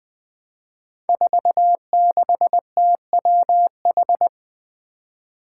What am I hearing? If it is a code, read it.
Morse code, heard as a keyed tone, about 20 wpm, 700 Hz. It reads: 46TWH
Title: Five Challenging Letter-Number Combinations 20wpm